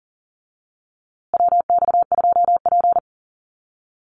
Morse code, heard as a keyed tone, about 40 wpm, 700 Hz. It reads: WX2P
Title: USA Call Signs Repeated 3x 40wpm